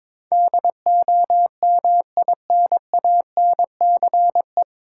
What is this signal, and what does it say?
Morse code, heard as a keyed tone, about 22 wpm, 700 Hz. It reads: DOMINANCE
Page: Morse Code Ninja practice 22wpm